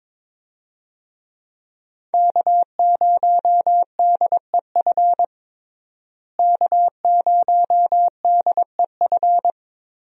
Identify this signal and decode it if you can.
Morse code, heard as a keyed tone, about 22 wpm, 700 Hz. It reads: K0DEF K0DEF
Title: USA Call Signs - 5 Characters - Repeated 2x - 22wpm